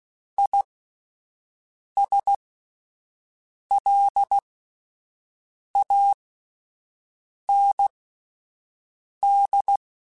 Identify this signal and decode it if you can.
Morse code, heard as a keyed tone, about 16 wpm, spaced as for 5 wpm, 800 Hz. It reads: ISLAND